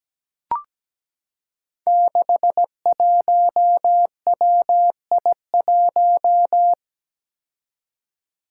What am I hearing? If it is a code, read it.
Morse code, heard as a keyed tone, about 17 wpm, 700 Hz. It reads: 61WI1
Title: Five Challenging Letter-Number Combinations 17wpm